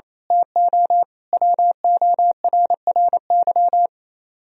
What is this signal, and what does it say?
Morse code, heard as a keyed tone, about 27 wpm, 700 Hz. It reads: TO WORRY
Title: Speed-Racing - Sets of 5 Words 28wpm